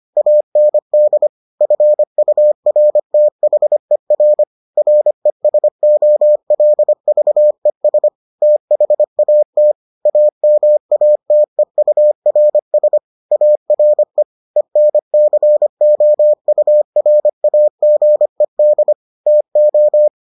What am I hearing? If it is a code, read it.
Morse code, heard as a keyed tone, about 25 wpm, 600 Hz. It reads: AND FURTHER RESOLVES THAT AMATEURS ARE ENCOURAGED TO